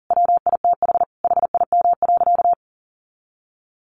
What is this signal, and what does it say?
Morse code, heard as a keyed tone, about 40 wpm, 700 Hz. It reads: WITH HIM.
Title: Sentences from Top 1100 Words 40wpm